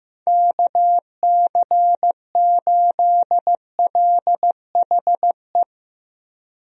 Morse code, heard as a keyed tone, about 15 wpm, 700 Hz. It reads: KC8LHE
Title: USA Call Signs - 6 Characters - Repeated 2x - 15wpm